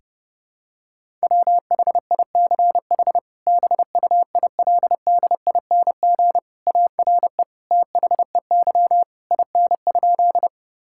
Morse code, heard as a keyed tone, about 30 wpm, 700 Hz. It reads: WHICH BUILDING ARE THEY IN?
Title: Sentences from Top 1000 Words - Review Part 5 - 30wpm